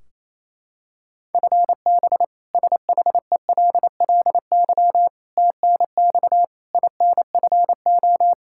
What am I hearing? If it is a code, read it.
Morse code, heard as a keyed tone, about 28 wpm, 700 Hz. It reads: FB SHELLY TNX INFO
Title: K1USN SST QSO Practice 28wpm